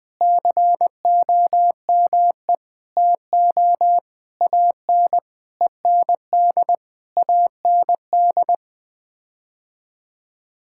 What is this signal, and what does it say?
Morse code, heard as a keyed tone, about 21 wpm, 700 Hz. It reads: COME TO AN END AND
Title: Sets of 9 Words - 20wpm